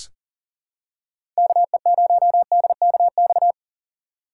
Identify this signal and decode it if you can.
Morse code, heard as a keyed tone, about 40 wpm, 700 Hz. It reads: KE0DKX
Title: USA Call Signs - 6 Characters - 3x Character Spacing - 40wpm